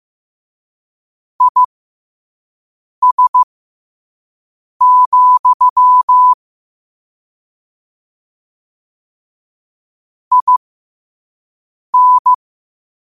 Morse code, heard as a keyed tone, about 15 wpm, spaced as for 5 wpm, 1000 Hz. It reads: IS, IN
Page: New quotes every day in morse code at 5 Words per minute.